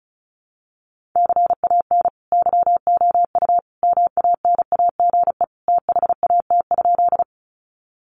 Morse code, heard as a keyed tone, about 34 wpm, 700 Hz. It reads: CAN YOU MANAGE THAT?
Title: Sentences from Top 1400 Words 35wpm